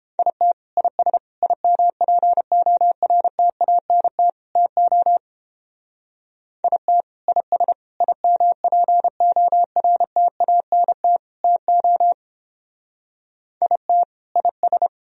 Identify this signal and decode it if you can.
Morse code, heard as a keyed tone, about 28 wpm, 700 Hz. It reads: IT IS IMPORTANT TO IT IS IMPORTANT TO IT IS